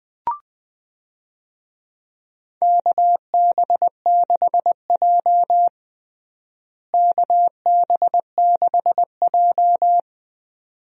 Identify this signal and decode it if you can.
Morse code, heard as a keyed tone, about 20 wpm, 700 Hz. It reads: KB6J KB6J
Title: USA Call Signs - 4 Characters - Repeated 2x - 20wpm